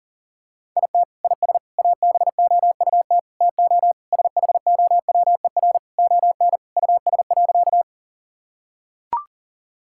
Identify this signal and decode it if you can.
Morse code, heard as a keyed tone, about 37 wpm, 700 Hz. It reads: IT IS ABOUT TO SHOWER ON US.
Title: Sentences from Top 1600 Words 40wpm